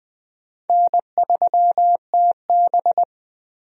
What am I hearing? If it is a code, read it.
Morse code, heard as a keyed tone, about 20 wpm, 700 Hz. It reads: N3TB
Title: USA Call Signs - 4 Characters - Repeated 3x - 20wpm